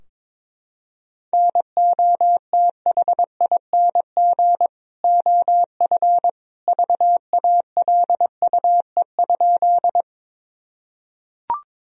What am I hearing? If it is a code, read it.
Morse code, heard as a keyed tone, about 22 wpm, 700 Hz. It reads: NOTHING OF VALUE?
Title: Sentences from Top 1000 Words - Review Part 3 - 22wpm